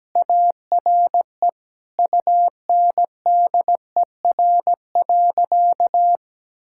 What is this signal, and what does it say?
Morse code, heard as a keyed tone, about 17 wpm, 700 Hz. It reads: ARE UNDER.
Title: Sentences from Top 1600 Words 17wpm